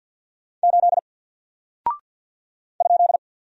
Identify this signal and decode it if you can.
Morse code, heard as a keyed tone, about 48 wpm, 700 Hz. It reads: 8 ?
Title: Single Character (Letter:Number:Punctuation:Prosign) 50wpm